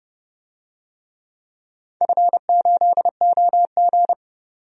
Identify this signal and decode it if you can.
Morse code, heard as a keyed tone, about 30 wpm, 700 Hz. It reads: F8OG
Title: DXCC-100 Call Signs - 4 Characters - 30wpm